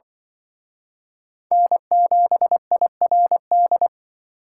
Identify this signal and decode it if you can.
Morse code, heard as a keyed tone, about 24 wpm, 700 Hz. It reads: N7IRD
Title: Speed Racing - USA 1x3 Call Signs 20wpm